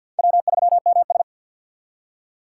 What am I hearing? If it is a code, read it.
Morse code, heard as a keyed tone, about 66 wpm, spaced as for 58 wpm, 700 Hz. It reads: W3MS
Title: Speed Racing - USA 1x2 Call Signs 50wpm